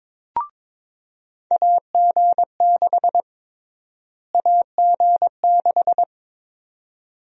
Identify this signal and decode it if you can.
Morse code, heard as a keyed tone, about 22 wpm, 700 Hz. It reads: AG6 AG6
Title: USA Call Sign Prefixes - Repeated 2x - 22wpm